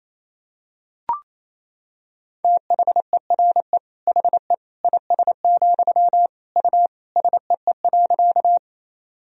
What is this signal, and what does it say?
Morse code, heard as a keyed tone, about 28 wpm, 700 Hz. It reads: THERE HE IS, U SEE.
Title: Sentences from Top 100 Words and Abbreviations 28wpm